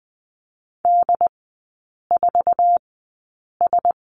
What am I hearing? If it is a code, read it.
Morse code, heard as a keyed tone, about 20 wpm, spaced as for 8 wpm, 700 Hz. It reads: D4S
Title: 133 Lesson 22e - Question Mark - Three Characters 20wpm